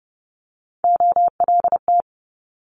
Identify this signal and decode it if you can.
Morse code, heard as a keyed tone, about 30 wpm, 700 Hz. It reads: OLT
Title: Multi-letter Phonemes Quick Fire 30wpm